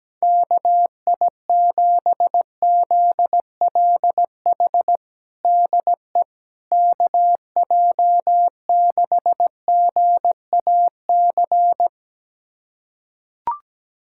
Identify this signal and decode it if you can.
Morse code, heard as a keyed tone, about 17 wpm, 700 Hz. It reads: KI7ZLH DE KJ6GAC
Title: USA Callsign DE Callsign - 6 Characters - 17wpm